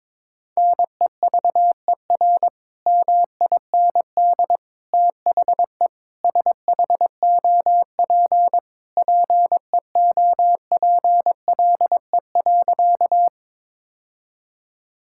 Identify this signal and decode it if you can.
Morse code, heard as a keyed tone, about 22 wpm, 700 Hz. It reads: NEVER MIND THE SHOP PEOPLE.
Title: Sentences from Top 1600 Words 22wpm